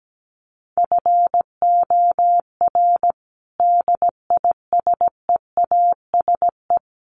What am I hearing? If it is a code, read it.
Morse code, heard as a keyed tone, about 17 wpm, 700 Hz. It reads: FOR DISEASE
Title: Sets of 5 Words Encore2 17wpm